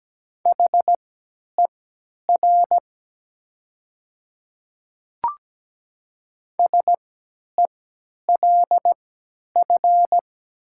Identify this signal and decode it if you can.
Morse code, heard as a keyed tone, about 17 wpm, spaced as for 10 wpm, 700 Hz. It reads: HER SELF
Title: 3x Character Spacing - Top 200 Words - 17wpm